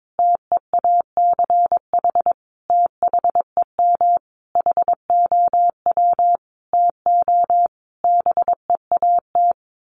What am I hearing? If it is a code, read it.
Morse code, heard as a keyed tone, about 22 wpm, 700 Hz. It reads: TEACH THEM HOW TO BEAT
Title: Sentences from Top 1300 Words 22wpm